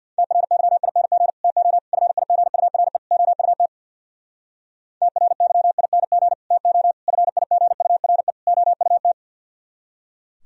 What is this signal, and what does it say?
Morse code, heard as a keyed tone, about 64 wpm, 700 Hz. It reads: TRYING TO FIGURE OUT TRYING TO FIGURE OUT